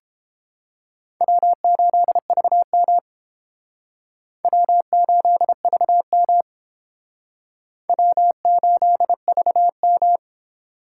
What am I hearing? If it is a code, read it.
Morse code, heard as a keyed tone, about 30 wpm, 700 Hz. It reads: W8VM W8VM W8VM